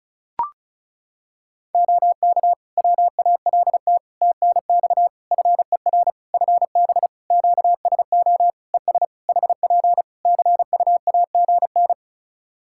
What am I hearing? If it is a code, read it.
Morse code, heard as a keyed tone, about 35 wpm, 700 Hz. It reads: OK WALT TNX FER FB QSO ES HP CUAGN
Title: Practice QSO - Exchange 4 - Ending - 35wpm